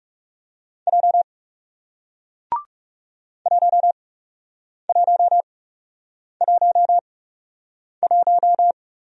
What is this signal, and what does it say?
Morse code, heard as a keyed tone, about 39 wpm, spaced as for 8 wpm, 700 Hz. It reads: J 1111